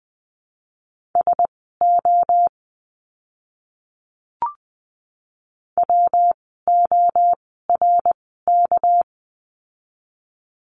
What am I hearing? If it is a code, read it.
Morse code, heard as a keyed tone, about 20 wpm, spaced as for 14 wpm, 700 Hz. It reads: SO WORK
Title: 230 Lesson 37f - Letter x - Words 20wpm